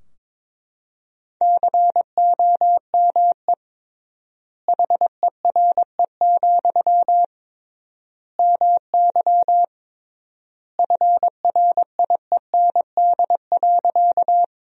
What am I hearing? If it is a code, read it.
Morse code, heard as a keyed tone, about 22 wpm, 700 Hz. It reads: COME HERE, MY FRIEND.
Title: 3x Word-Spacing - Sentences from Top 300 Words - 22wpm